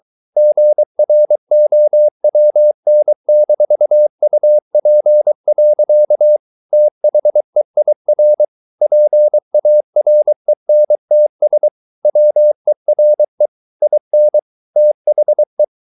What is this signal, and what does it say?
Morse code, heard as a keyed tone, about 23 wpm, 600 Hz. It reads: GROWN-UP. THEIR PARENTS WERE IN THE